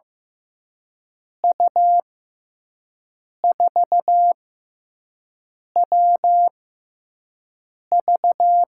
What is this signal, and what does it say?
Morse code, heard as a keyed tone, about 15 wpm, spaced as for 5 wpm, 700 Hz. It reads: U4WV
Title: USA Call Signs - 5 Characters - 6x Character Spacing - 15wpm